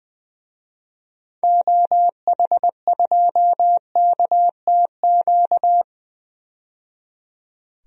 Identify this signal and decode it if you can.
Morse code, heard as a keyed tone, about 21 wpm, 700 Hz. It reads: OH2KTQ
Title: DXCC-100 Call Signs - 6 Characters - 20wpm